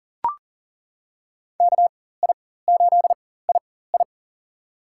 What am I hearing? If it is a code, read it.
Morse code, heard as a keyed tone, about 40 wpm, spaced as for 19 wpm, 700 Hz. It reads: KI8II
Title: USA Call Signs - 4x Character Spacing - 40wpm